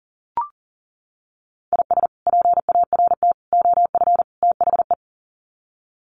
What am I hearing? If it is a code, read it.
Morse code, heard as a keyed tone, about 40 wpm, 700 Hz. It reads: IS PART OF THE
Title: Sets of 4 Words 40wpm